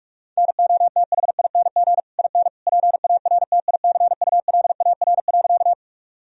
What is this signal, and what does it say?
Morse code, heard as a keyed tone, about 46 wpm, 700 Hz. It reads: NOTHING IN PARTICULAR.
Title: Sentences from Top 1000 Words - Review Part 4 - 45wpm